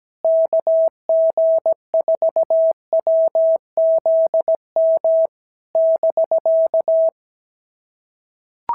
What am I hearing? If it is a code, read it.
Morse code, heard as a keyed tone, about 16 wpm, 650 Hz. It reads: KG4WZM <BK>
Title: SOTA QSO Practice 17wpm